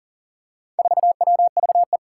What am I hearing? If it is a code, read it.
Morse code, heard as a keyed tone, about 42 wpm, 700 Hz. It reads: 4WVE
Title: Four Challenging Letter-Number Combinations 40wpm